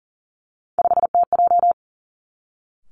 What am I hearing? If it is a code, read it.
Morse code, heard as a keyed tone, about 39 wpm, 700 Hz. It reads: 5TJ
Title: Three Challenging Letter-Number Combinations 40wpm